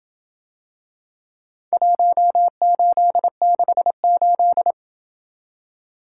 Morse code, heard as a keyed tone, about 27 wpm, 700 Hz. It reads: 1868